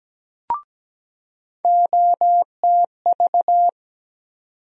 Morse code, heard as a keyed tone, about 17 wpm, 700 Hz. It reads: OTV
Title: USA Call Sign Postfixes - 3 Letters 17wpm